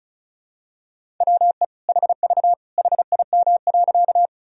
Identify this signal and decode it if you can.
Morse code, heard as a keyed tone, about 34 wpm, 700 Hz. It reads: WE HV HIM.
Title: Sentences from Top 100 Words and Abbreviations 35wpm